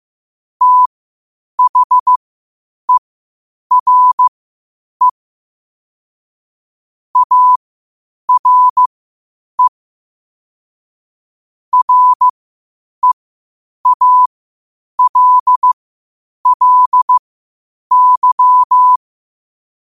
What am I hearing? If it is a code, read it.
Morse code, heard as a keyed tone, about 15 wpm, spaced as for 8 wpm, 1000 Hz. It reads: THERE ARE REALLY
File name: qotdfiles / 8wpm